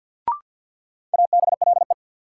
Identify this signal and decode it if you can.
Morse code, heard as a keyed tone, about 47 wpm, 700 Hz. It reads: ABLE
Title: Sentences from Top 700 Words - New Word Drill - 50wpm